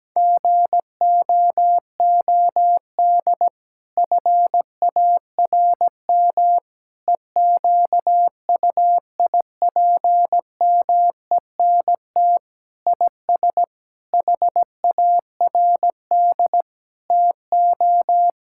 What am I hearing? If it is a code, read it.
Morse code, heard as a keyed tone, about 17 wpm, 700 Hz. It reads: GOOD FARM EQUIPMENT IS HARD TO
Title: Sentences from Top 1000 Words - Review Part 5 - 17wpm